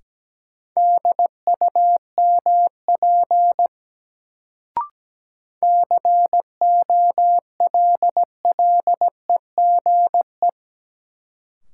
Morse code, heard as a keyed tone, about 17 wpm, 700 Hz. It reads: DUMP COLLEGE
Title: Sentences from Top 1300 Words - New Word Drill - 17wpm